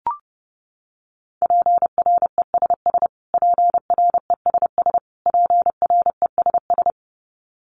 Morse code, heard as a keyed tone, about 30 wpm, 700 Hz. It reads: PRESS PRESS PRESS
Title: Top 500 Words Repeated 30wpm